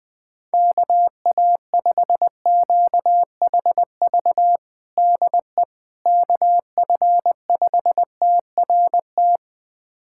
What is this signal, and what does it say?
Morse code, heard as a keyed tone, about 20 wpm, 700 Hz. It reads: KA5QHV DE KF5TRT
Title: USA Callsign DE Callsign - 6 Characters - 20wpm